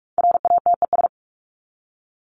Morse code, heard as a keyed tone, about 44 wpm, 700 Hz. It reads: RATES
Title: ES:S Suffix - Top 100 Words with Base Words 45wpm